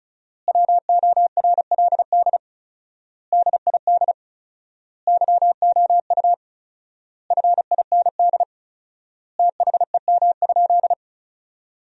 Morse code, heard as a keyed tone, about 35 wpm, 700 Hz. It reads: WORLD DID YOU FIND THEM?
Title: 4x Word-Spacing - Sentences from Top 200 Words - 35wpm